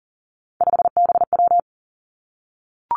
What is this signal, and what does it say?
Morse code, heard as a keyed tone, about 39 wpm, 700 Hz. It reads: 5BW
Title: Three Challenging Letter-Number Combinations 40wpm